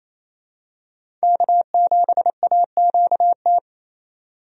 Morse code, heard as a keyed tone, about 28 wpm, 700 Hz. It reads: K7AQT
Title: USA Call Signs - 5 Characters - 28wpm